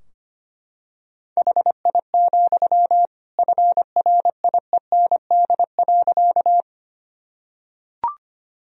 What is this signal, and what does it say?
Morse code, heard as a keyed tone, about 25 wpm, 700 Hz. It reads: HI, FRIEND.
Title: Sentences from Top 1000 Words - Review Part 3 - 25wpm